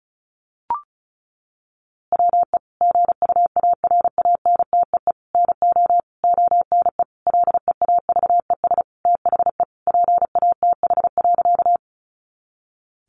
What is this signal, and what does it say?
Morse code, heard as a keyed tone, about 34 wpm, 700 Hz. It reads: WE GUARANTEE NO ONE LEAVES THE PATH.
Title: A Sound Of Thunder 35wpm